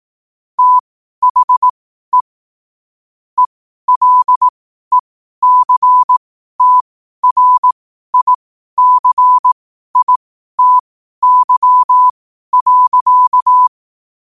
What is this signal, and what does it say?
Morse code, heard as a keyed tone, about 18 wpm, spaced as for 12 wpm, 1000 Hz. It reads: THE ELECTRICITY.
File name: qotdfiles / 12wpm